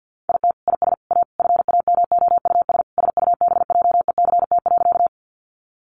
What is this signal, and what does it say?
Morse code, heard as a keyed tone, about 50 wpm, 700 Hz. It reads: IT IS A FAMOUS SUBJECT.
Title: Sentences from Top 1500 Words 50wpm